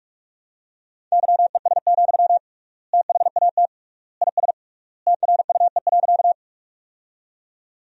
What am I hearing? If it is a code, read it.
Morse code, heard as a keyed tone, about 46 wpm, 700 Hz. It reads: YES, THAT IS TRUE.
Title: 3x Word-Spacing - Sentences from Top 500 Words - 45wpm